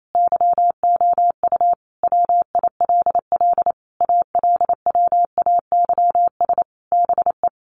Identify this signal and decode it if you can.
Morse code, heard as a keyed tone, about 28 wpm, 700 Hz. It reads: YOU WILL ALWAYS BE
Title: Sentences from Top 1500 Words 28wpm